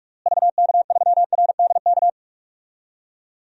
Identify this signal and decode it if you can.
Morse code, heard as a keyed tone, about 41 wpm, 700 Hz. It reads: VK3RDK
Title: DXCC-100 Call Signs - 6 Characters - Repeated 2x - 45wpm